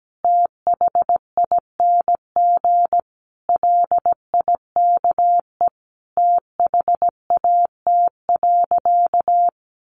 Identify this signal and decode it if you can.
Morse code, heard as a keyed tone, about 17 wpm, 700 Hz. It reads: THING LIKE THAT.
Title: A Sound Of Thunder 17wpm